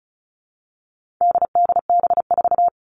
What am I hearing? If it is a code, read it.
Morse code, heard as a keyed tone, about 34 wpm, 700 Hz. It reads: DDB4
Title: All Cut Numbers - 4-Digit 35wpm